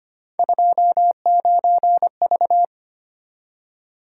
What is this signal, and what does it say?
Morse code, heard as a keyed tone, about 26 wpm, 700 Hz. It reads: 29V
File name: Three Letter-Number Combinations - 25wpm